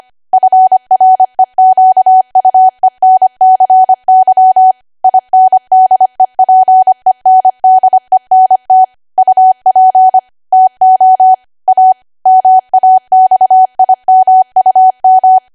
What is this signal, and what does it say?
Morse code, heard as a keyed tone, about 25 wpm, 750 Hz. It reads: FREQUENCY INDEPENDENT UP TO A MAXIMUM
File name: Archive / 25 WPM / 25WPM